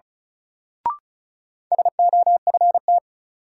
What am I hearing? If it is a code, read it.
Morse code, heard as a keyed tone, about 36 wpm, 700 Hz. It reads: SOFT
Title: Sentences from Top 1500 Words - New Word Drill Alt - 35wpm